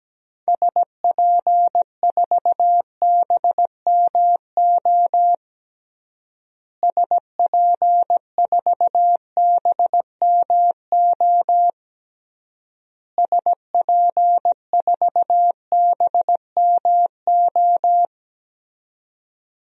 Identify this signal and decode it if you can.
Morse code, heard as a keyed tone, about 17 wpm, 700 Hz. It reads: SP4BMO SP4BMO SP4BMO
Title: DXCC-100 Call Signs - 6 Characters - Repeated 3x - 17wpm